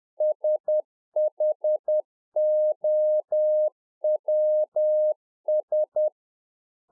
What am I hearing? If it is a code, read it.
Morse code, heard as a keyed tone, about 10 wpm, 610 Hz. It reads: SHOWS
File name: 2011-04-mp3 / RICK JJ0MPI 10WPM